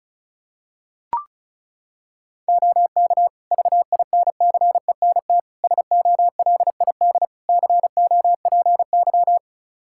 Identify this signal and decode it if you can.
Morse code, heard as a keyed tone, about 36 wpm, 700 Hz. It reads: OK VINCENT SOLID COPY
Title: Practice QSO - Exchange 3 - More Info - 35wpm